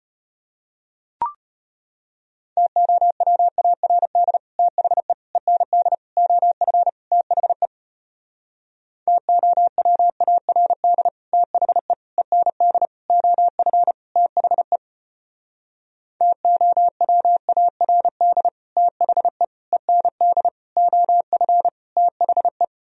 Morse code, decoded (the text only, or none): TOWARD THE END OF THE TOWARD THE END OF THE TOWARD THE END OF THE